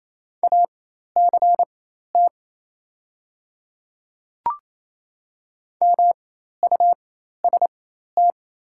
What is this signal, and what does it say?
Morse code, heard as a keyed tone, about 27 wpm, spaced as for 13 wpm, 690 Hz. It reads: ACT MUST
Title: 4x Character Spacing - Top 200 Words - 28wpm